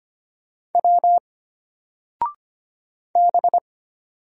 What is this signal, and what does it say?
Morse code, heard as a keyed tone, about 25 wpm, 700 Hz. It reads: W B
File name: Single-Letter-Number-25wpm